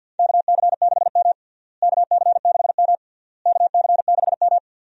Morse code, heard as a keyed tone, about 49 wpm, 700 Hz. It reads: KC6M KC6M KC6M
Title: USA Call Signs - 4 Characters - Repeated 3x - 50wpm